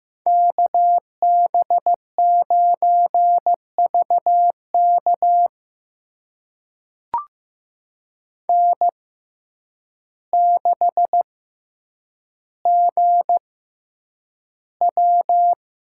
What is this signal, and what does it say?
Morse code, heard as a keyed tone, about 15 wpm, spaced as for 8 wpm, 700 Hz. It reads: KB9VK N6GW
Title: USA Call Signs - 5 Characters - 6x Character Spacing - 15wpm